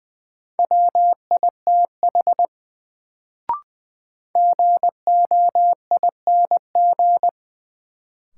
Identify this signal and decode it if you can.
Morse code, heard as a keyed tone, about 20 wpm, 700 Hz. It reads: WITH GOING
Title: Sets of 2 Words - New Word Drill - 20wpm